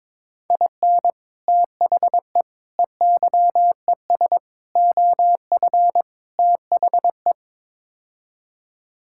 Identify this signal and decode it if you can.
Morse code, heard as a keyed tone, about 22 wpm, 700 Hz. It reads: IN THE EYES OF THE
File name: practice / 22wpm / Build A Phrase - 5-Word Phrases - 22wpm